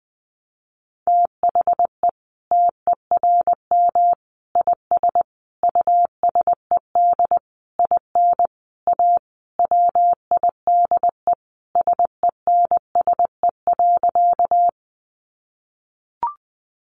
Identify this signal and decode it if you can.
Morse code, heard as a keyed tone, about 20 wpm, 700 Hz. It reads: THE TERM IS USED IN A WIDE SENSE.
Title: Sentences from Top 900 Words 20wpm